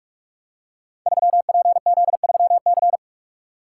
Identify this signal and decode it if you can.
Morse code, heard as a keyed tone, about 44 wpm, 700 Hz. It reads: 3PZ3C
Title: Code Groups - 5 Letter:Numbers - 45wpm